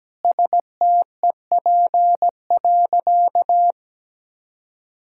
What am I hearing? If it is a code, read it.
Morse code, heard as a keyed tone, about 17 wpm, 700 Hz. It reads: STEP.
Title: Sentences from Top 1600 Words 17wpm